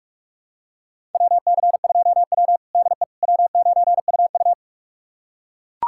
Morse code, heard as a keyed tone, about 42 wpm, 700 Hz. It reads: WC2W DE W9UU
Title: USA Callsign DE Callsign - 4 Characters - 45wpm